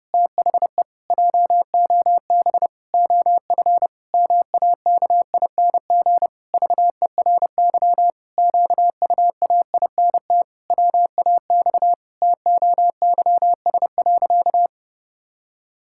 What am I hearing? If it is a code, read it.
Morse code, heard as a keyed tone, about 32 wpm, 700 Hz. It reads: THE JOB OF MAKING VERY QUAINT WAX TOYS.
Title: Pangram Challenge v2 - 30wpm